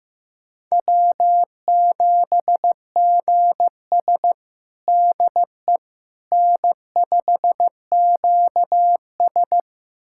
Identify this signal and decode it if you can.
Morse code, heard as a keyed tone, about 15 wpm, 700 Hz. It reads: W7GS DE N5QS
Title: USA Callsign DE Callsign - 4 Characters - 15wpm